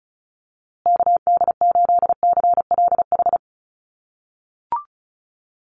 Morse code, heard as a keyed tone, about 34 wpm, 700 Hz. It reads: KD8CLH
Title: USA Call Signs Repeated 2x 35wpm